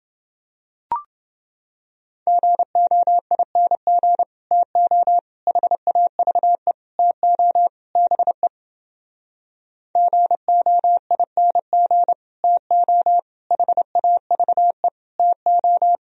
Morse code, heard as a keyed tone, about 30 wpm, 700 Hz. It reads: GOING TO HAVE TO BE GOING TO HAVE TO